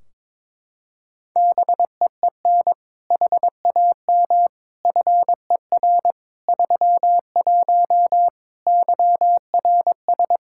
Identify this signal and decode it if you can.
Morse code, heard as a keyed tone, about 22 wpm, 700 Hz. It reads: BEEN HAM FER 31 YRS
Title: Practice QSO - Exchange 3 - More Info - 22wpm